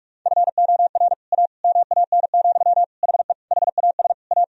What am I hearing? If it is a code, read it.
Morse code, heard as a keyed tone, about 46 wpm, 700 Hz. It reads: FOR A MAN, HE HAS A
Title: Sentences from Top 1000 Words - Review Part 2 - 45wpm